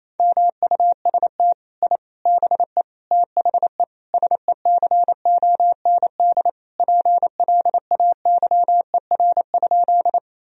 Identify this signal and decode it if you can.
Morse code, heard as a keyed tone, about 28 wpm, 700 Hz. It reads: MUST I BE THE SECOND PLAYER?
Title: Sentences from Top 1000 Words - Review Part 4 - 28wpm